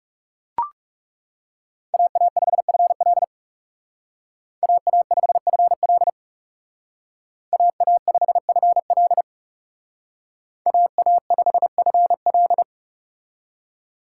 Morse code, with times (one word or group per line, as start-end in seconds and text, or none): 1.9-3.2 AA5FL
4.6-6.1 AA5FL
7.5-9.2 AA5FL
10.7-12.6 AA5FL